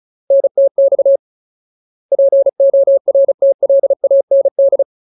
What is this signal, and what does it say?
Morse code, heard as a keyed tone, about 34 wpm, 550 Hz. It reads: NTX PORTLAND
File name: Ultimate Word List / 35WPM Ultimate